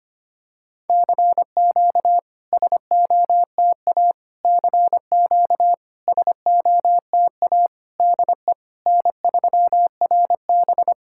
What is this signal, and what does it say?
Morse code, heard as a keyed tone, about 25 wpm, 690 Hz. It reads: CQ SOTA CQ SOTA DE N3RB
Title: SOTA QSO Practice 25wpm